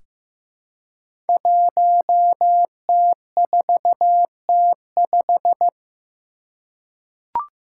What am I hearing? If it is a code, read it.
Morse code, heard as a keyed tone, about 15 wpm, 700 Hz. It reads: 1T4T5
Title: Five Challenging Letter-Number Combinations 15wpm